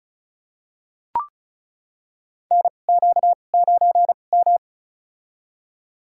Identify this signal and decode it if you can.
Morse code, heard as a keyed tone, about 35 wpm, spaced as for 25 wpm, 700 Hz. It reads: NQ9M
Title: USA Call Signs - 4 Characters - 2x Character Spacing - 35wpm